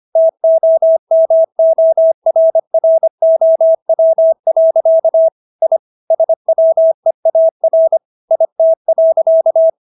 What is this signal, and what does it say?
Morse code, heard as a keyed tone, about 25 wpm, 650 Hz. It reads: TOMORROW. I SWEAR IT.